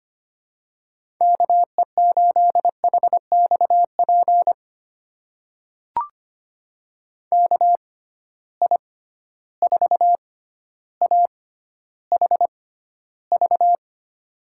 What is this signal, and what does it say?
Morse code, heard as a keyed tone, about 25 wpm, spaced as for 13 wpm, 700 Hz. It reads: KE8HXP KI4AHV
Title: USA Call Signs - 6 Characters - 6x Character Spacing - 25wpm